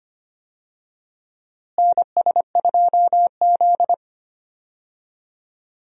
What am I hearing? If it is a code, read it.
Morse code, heard as a keyed tone, about 25 wpm, 700 Hz. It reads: NS2Z
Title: USA Call Signs - 4 Characters - Repeated 2x - 25wpm